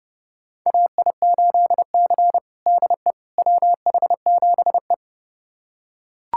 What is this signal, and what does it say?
Morse code, heard as a keyed tone, about 32 wpm, 700 Hz. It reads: AI8C DE WH7E
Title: USA Callsign DE Callsign - 4 Characters - 30wpm